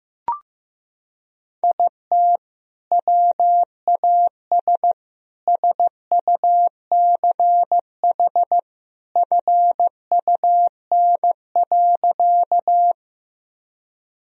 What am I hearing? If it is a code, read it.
Morse code, heard as a keyed tone, about 15 wpm, 700 Hz. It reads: IT WAS SUCH FUN.
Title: Sentences from Top 1000 Words - Review Part 4 - 15wpm